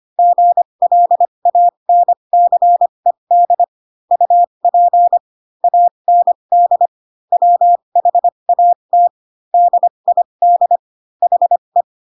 Morse code, heard as a keyed tone, about 25 wpm, spaced as for 22 wpm, 700 Hz. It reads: GLANCED UP AND WHAT DID HE
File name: Pinocchio / Pinocchio ch3 18 25 ik1hge